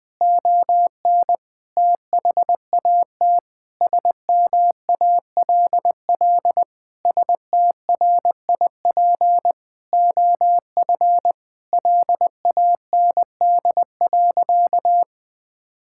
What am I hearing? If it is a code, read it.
Morse code, heard as a keyed tone, about 19 wpm, 700 Hz. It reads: ON THAT SMALL STRIP OF LAND.
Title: Sentences from Top 1600 Words 20wpm